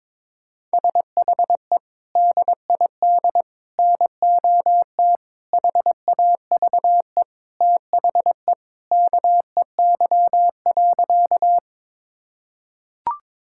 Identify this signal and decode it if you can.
Morse code, heard as a keyed tone, about 22 wpm, 700 Hz. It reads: SHE DID NOT HAVE THE KEY.
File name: practice / 22wpm / Sentences with 4-Word Phrases - 22wpm